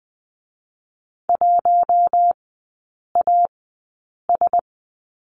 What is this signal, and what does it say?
Morse code, heard as a keyed tone, about 20 wpm, spaced as for 8 wpm, 700 Hz. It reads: 1AS
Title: 041 Lesson 8e - Number 1 - Three Characters 20wpm